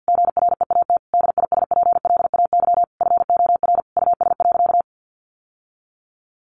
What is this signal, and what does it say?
Morse code, heard as a keyed tone, about 50 wpm, 700 Hz. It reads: GREAT DISPLAY FOR US.
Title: Sentences from Top 1200 Words 50wpm